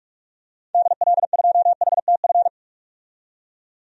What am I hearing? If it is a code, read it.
Morse code, heard as a keyed tone, about 41 wpm, 700 Hz. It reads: DL2HTF
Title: DXCC-100 Call Signs - 6 Characters - 45wpm